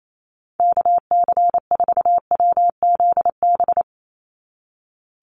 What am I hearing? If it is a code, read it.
Morse code, heard as a keyed tone, about 28 wpm, 700 Hz. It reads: KC4WZB
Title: NA SOTA Activator Callsigns 28wpm